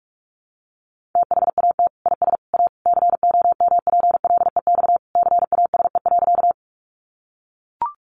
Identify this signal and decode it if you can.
Morse code, heard as a keyed tone, about 44 wpm, 700 Hz. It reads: THAT IS A COMPLEX CASE.
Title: Sentences from Top 900 Words 45wpm